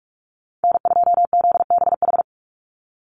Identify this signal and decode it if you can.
Morse code, heard as a keyed tone, about 44 wpm, 700 Hz. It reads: N2ZBH
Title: USA Call Signs 45wpm